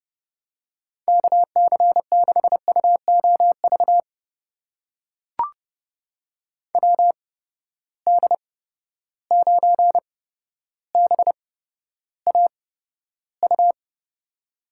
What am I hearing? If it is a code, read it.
Morse code, heard as a keyed tone, about 32 wpm, spaced as for 13 wpm, 700 Hz. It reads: KC6UOV WD9BAU
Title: USA Call Signs - 6 Characters - 8x Character Spacing - 30wpm